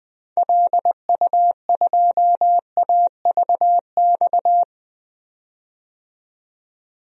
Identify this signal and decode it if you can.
Morse code, heard as a keyed tone, about 21 wpm, 700 Hz. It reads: LU2AVX
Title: DXCC-100 Call Signs - 6 Characters - 20wpm